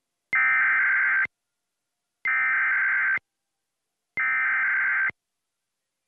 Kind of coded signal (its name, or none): Emergency Alert System (SAME) tones